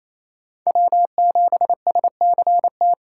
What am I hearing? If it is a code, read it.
Morse code, heard as a keyed tone, about 28 wpm, 700 Hz. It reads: W7SCT
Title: USA Call Signs - 3x Character Spacing - 28wpm